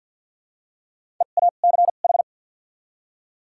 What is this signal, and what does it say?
Morse code, heard as a keyed tone, about 52 wpm, 700 Hz. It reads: E A C H
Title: 2x Character Spacing - Top 200 Words - 50wpm